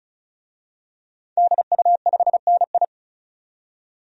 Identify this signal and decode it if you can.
Morse code, heard as a keyed tone, about 34 wpm, 700 Hz. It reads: DU5NI
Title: DXCC-100 Call Signs - 5 Characters - 35wpm